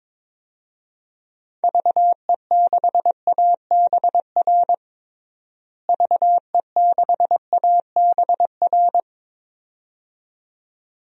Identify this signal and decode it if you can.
Morse code, heard as a keyed tone, about 22 wpm, 700 Hz. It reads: VE6ABR VE6ABR
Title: DXCC-100 Call Signs - 6 Characters - Repeated 2x - 22wpm